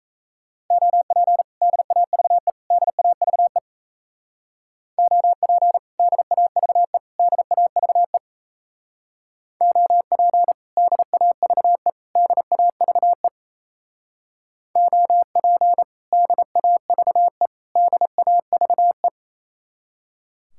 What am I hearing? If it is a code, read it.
Morse code, heard as a keyed tone, about 37 wpm, 700 Hz. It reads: OP DAVE DAVE OP DAVE DAVE OP DAVE DAVE OP DAVE DAVE